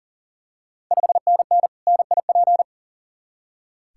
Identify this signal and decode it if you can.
Morse code, heard as a keyed tone, about 38 wpm, 690 Hz. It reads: 5NN NIP
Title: Ontario QSO Party Exchange 40wpm